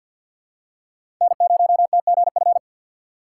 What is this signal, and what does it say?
Morse code, heard as a keyed tone, about 48 wpm, 700 Hz. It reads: N0TGF
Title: USA Call Signs - 5 Characters - 2x Character Spacing - 50wpm